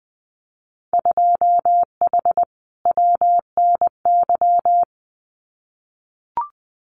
Morse code, heard as a keyed tone, about 20 wpm, 700 Hz. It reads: 2H WNY
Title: Winter Field Day Exchanges Encore 20wpm